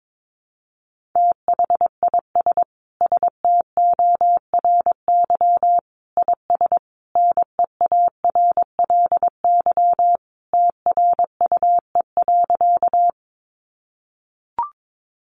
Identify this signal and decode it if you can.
Morse code, heard as a keyed tone, about 22 wpm, 700 Hz. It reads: THIS STORY IS NEARLY TRUE.
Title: Sentences from Top 1300 Words 22wpm